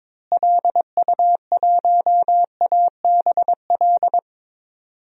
Morse code, heard as a keyed tone, about 22 wpm, 700 Hz. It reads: LU1ABL
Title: DXCC-100 Call Signs - 6 Characters - Repeated 3x - 22wpm